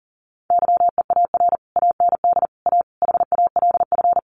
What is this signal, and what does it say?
Morse code, heard as a keyed tone, about 40 wpm, 700 Hz. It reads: YEAR AND A HALF
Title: Sets of 5 Words Encore 40wpm